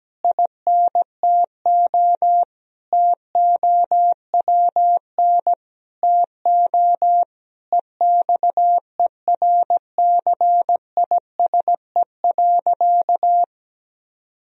Morse code, heard as a keyed tone, about 17 wpm, 700 Hz. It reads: INTO TOWN TO EXERCISE.
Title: Sentences from Top 1000 Words - Review Part 4 - 17wpm